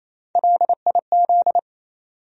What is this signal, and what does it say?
Morse code, heard as a keyed tone, about 28 wpm, 700 Hz. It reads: LIZ
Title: USA Common Names - Max 3 Letters - 28wpm